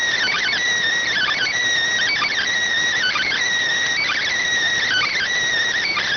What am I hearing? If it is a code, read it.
Emergency Alert System (SAME) tones (header bursts), heard as harsh buzzing bursts. It is heard from 1.1 to 5.3 s.